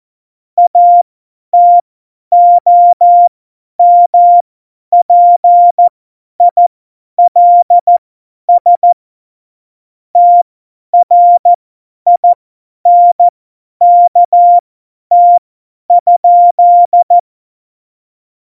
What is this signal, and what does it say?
Morse code, heard as a keyed tone, about 14 wpm, spaced as for 10 wpm, 700 Hz. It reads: ATOMPILS TRINKT?
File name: Morsekurs / Witze / Witz 08. Oktober 2019, 50 F70